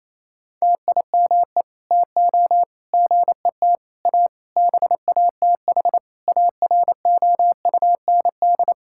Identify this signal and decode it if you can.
Morse code, heard as a keyed tone, about 28 wpm, 700 Hz. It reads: TIME TO GET A BATH AROUND
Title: Sentences from Top 1600 Words 28wpm